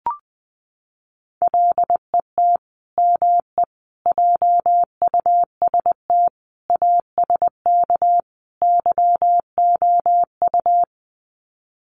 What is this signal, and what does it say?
Morse code, heard as a keyed tone, about 20 wpm, 700 Hz. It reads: LET ME JUST ASK YOU
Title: Sets of 5 Words Encore2 20wpm